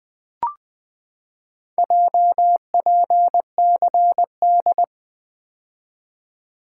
Morse code, heard as a keyed tone, about 20 wpm, 700 Hz. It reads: JPCD
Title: Four Letter-Number Combinations - 20wpm